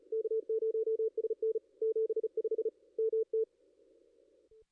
Morse code, heard as a keyed tone, about 40 wpm, 440 Hz, with radio noise. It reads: K0SN 75 MT
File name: contests / ss-cw-k5tr-op-n5rz -2300